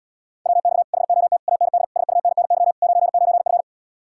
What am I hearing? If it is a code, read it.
Morse code, heard as a keyed tone, about 77 wpm, 700 Hz. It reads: WX HOT AND SUNNY 99F